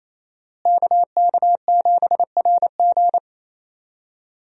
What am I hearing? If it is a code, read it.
Morse code, heard as a keyed tone, about 28 wpm, 700 Hz. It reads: KK7RG
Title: USA Call Signs - 5 Characters - Repeated 3x - 28wpm